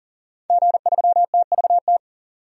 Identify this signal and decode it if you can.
Morse code, heard as a keyed tone, about 40 wpm, 700 Hz. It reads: G3TVT
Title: DXCC-100 Call Signs - 5 Characters - 40wpm